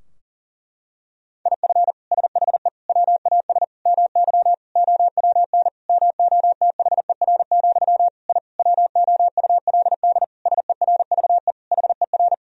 Morse code, heard as a keyed tone, about 40 wpm, 700 Hz. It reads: IF SHE WAS MY OWN MOTHER, I WOULD SERVE HER
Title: Sentences from Top 1000 Words - Review Part 2 - 40wpm